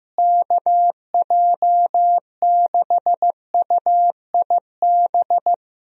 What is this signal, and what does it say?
Morse code, heard as a keyed tone, about 15 wpm, 700 Hz. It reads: KJ6UIB
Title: USA Call Signs - 6 Characters - Repeated 3x - 15wpm